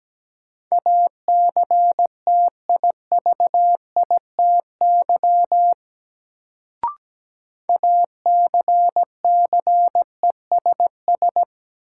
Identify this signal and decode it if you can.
Morse code, heard as a keyed tone, about 17 wpm, 700 Hz. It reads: ACTIVITY ACCESS
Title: Sentences from Top 1000 Words - Review Part 4 - New Word Drill - 17wpm